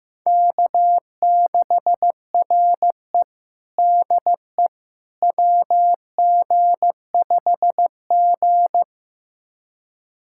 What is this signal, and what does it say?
Morse code, heard as a keyed tone, about 14 wpm, 700 Hz. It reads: K6RE DE WG5G
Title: USA Callsign DE Callsign - 4 Characters - 15wpm